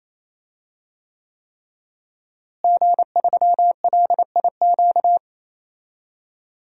Morse code, heard as a keyed tone, about 27 wpm, 700 Hz. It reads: G3LIQ
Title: DXCC-100 Call Signs - 5 Characters - Repeated 3x - 28wpm